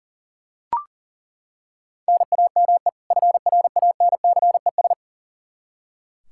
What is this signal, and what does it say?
Morse code, heard as a keyed tone, about 40 wpm, 700 Hz. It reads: NAME FRANCES
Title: Practice QSO - Exchange 1 - Beginning - 40wpm